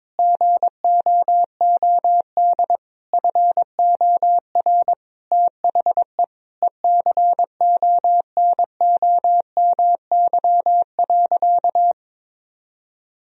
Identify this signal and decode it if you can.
Morse code, heard as a keyed tone, about 22 wpm, 700 Hz. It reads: GOOD FOR THE ECONOMY.
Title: Sentences from Top 1000 Words - Review Part 4 - 22wpm